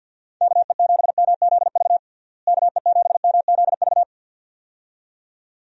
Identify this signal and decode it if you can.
Morse code, heard as a keyed tone, about 53 wpm, 700 Hz. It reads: KE7MZV KE7MZV
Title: USA Call Signs - 6 Characters - Repeated 2x - 50wpm